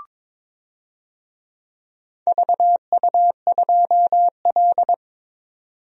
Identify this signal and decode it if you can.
Morse code, heard as a keyed tone, about 22 wpm, 700 Hz. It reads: VU2L
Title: DXCC-100 Call Signs - 4 Characters - 22wpm